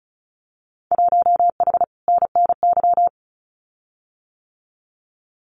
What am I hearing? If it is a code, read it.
Morse code, heard as a keyed tone, about 34 wpm, 700 Hz. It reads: 1H NNY
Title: Winter Field Day Exchanges 35wpm